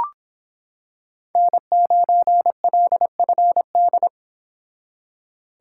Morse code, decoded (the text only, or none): N9LFD